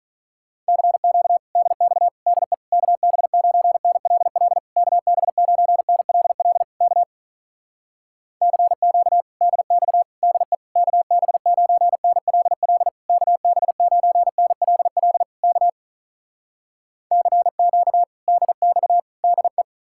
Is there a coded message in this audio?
Yes: Morse code from 0.7 to 19.6 s, heard as a keyed tone, about 48 wpm, 700 Hz. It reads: CQ DX DE KB9NLL KB9NLL K CQ DX DE KB9NLL KB9NLL K CQ DX DE